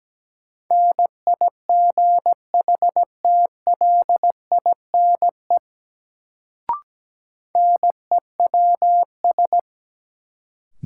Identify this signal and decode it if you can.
Morse code, heard as a keyed tone, about 17 wpm, 700 Hz. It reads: NIGHTLINE NEWS
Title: Sets of 5 Words - New Word Drill - 17wpm